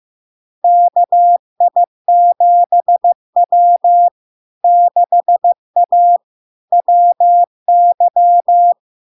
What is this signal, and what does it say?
Morse code, heard as a keyed tone, about 15 wpm, 700 Hz. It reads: KI7W 6A WY
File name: LARK-FD-2022-Log-in-CW-15wpm-700-Hz